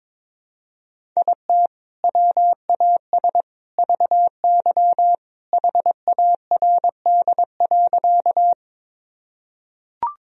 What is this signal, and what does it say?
Morse code, heard as a keyed tone, about 22 wpm, 700 Hz. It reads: IT WAS VY HARD.
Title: Sentences from Top 300 Words and Abbreviations 22wpm